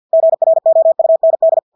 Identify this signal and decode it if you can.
Morse code, heard as a keyed tone, about 50 wpm, 650 Hz. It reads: GROUND
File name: Engelske ord 50 wpm